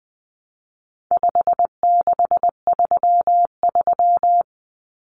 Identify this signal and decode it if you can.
Morse code, heard as a keyed tone, about 19 wpm, 700 Hz. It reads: 5633
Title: Numbers - 4-Digit 20wpm